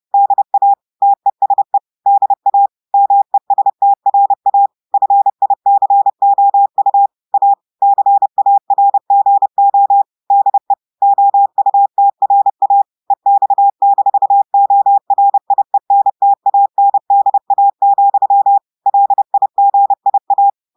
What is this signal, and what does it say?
Morse code, heard as a keyed tone, about 30 wpm, 800 Hz. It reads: DA TESE DA MESTRA FICOU A CARGO DE OUTRA EX-ORIENTANDA, LIGIA
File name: audios / portugais / portugais texte en clair 1 vitesse 30 mots minutes